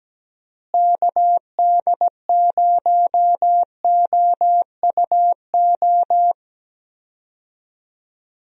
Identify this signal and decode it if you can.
Morse code, heard as a keyed tone, about 17 wpm, 700 Hz. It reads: KD0OUO
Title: USA Call Signs - 6 Characters - Repeated 3x - 17wpm